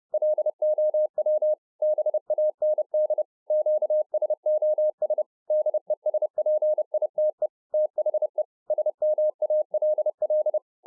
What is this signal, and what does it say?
Morse code, heard as a keyed tone, about 30 wpm, 610 Hz. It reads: LOW BAND QSOS DESPITE THE SMALL